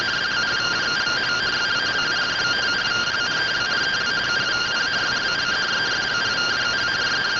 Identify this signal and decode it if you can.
Two-tone teleprinter signal, about 170 Hz shift.